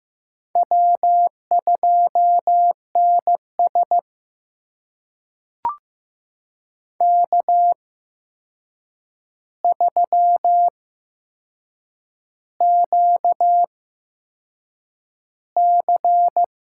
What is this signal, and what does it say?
Morse code, heard as a keyed tone, about 15 wpm, spaced as for 6 wpm, 700 Hz. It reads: W2NS K3QC
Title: USA Call Signs - 8x Character Spacing - 15wpm